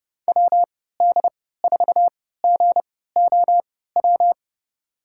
Morse code, heard as a keyed tone, about 30 wpm, spaced as for 17 wpm, 700 Hz. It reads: WD4GOW
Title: USA Call Signs - 6 Characters - 3x Character Spacing - 30wpm